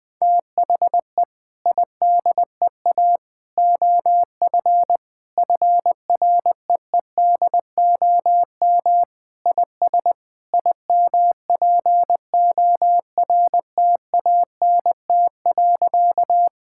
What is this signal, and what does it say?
Morse code, heard as a keyed tone, about 21 wpm, 700 Hz. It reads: THE IDEA OF FREEDOM IS IMPORTANT.
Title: Sentences with 3-Word Phrases - 20wpm